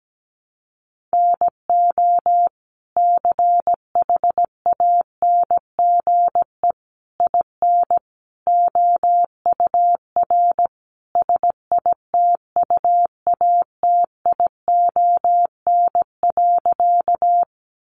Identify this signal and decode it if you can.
Morse code, heard as a keyed tone, about 17 wpm, 700 Hz. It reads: NO CHANGE IN OUR SITUATION.
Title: Sentences from Top 1200 Words 17wpm